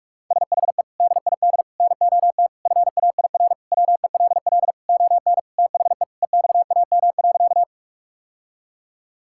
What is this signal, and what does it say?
Morse code, heard as a keyed tone, about 46 wpm, 690 Hz. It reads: SHE DID NOT FAIR WELL ON THE EXAM.
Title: Sentences from Top 1000 Words - Review Part 5 - 45wpm